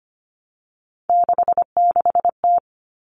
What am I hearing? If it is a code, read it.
Morse code, heard as a keyed tone, about 25 wpm, 700 Hz. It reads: 66T
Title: Common Cut Numbers - 3-Digit 25wpm